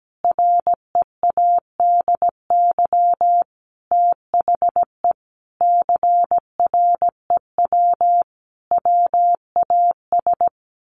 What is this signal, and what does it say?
Morse code, heard as a keyed tone, about 17 wpm, 700 Hz. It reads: READY THE CREW WAS
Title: Sentences from Top 1500 Words 17wpm